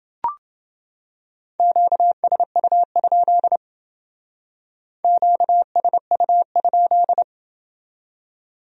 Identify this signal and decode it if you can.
Morse code, heard as a keyed tone, about 29 wpm, 700 Hz. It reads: QSU? QSU?